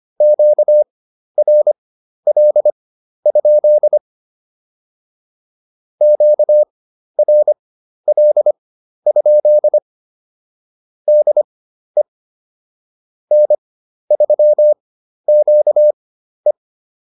Morse code, heard as a keyed tone, about 25 wpm, spaced as for 12 wpm, 600 Hz. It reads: QRL? QRL? DE N3QE